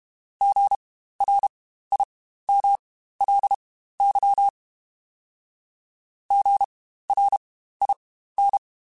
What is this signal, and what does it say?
Morse code, heard as a keyed tone, about 32 wpm, spaced as for 15 wpm, 800 Hz. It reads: GRIMLY GRIN
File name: Spaced Alphas / Spaced Alphas mp3s / Gs sp @30